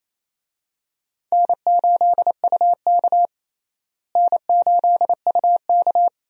Morse code, heard as a keyed tone, about 28 wpm, 700 Hz. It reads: N8UK N8UK
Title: USA Call Signs - 4 Characters - Repeated 3x - 28wpm